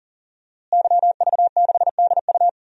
Morse code, heard as a keyed tone, about 40 wpm, 700 Hz. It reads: YV6DU
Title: DXCC-100 Call Signs - 5 Characters - Repeated 3x - 40wpm